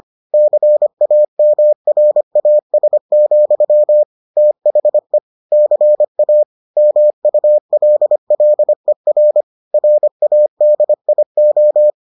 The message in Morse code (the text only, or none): CAMRAS, THE CA MULLER RADIO